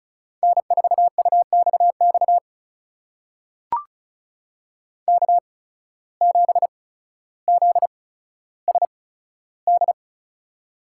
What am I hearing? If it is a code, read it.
Morse code, heard as a keyed tone, about 36 wpm, spaced as for 15 wpm, 700 Hz. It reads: N4UXX K7ZSD
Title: USA Call Signs - 5 Characters - 8x Character Spacing - 35wpm